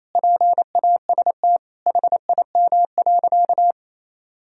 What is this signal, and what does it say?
Morse code, heard as a keyed tone, about 28 wpm, 700 Hz. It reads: PAST HIM.
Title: Sentences from Top 1000 Words - Review Part 3 - 28wpm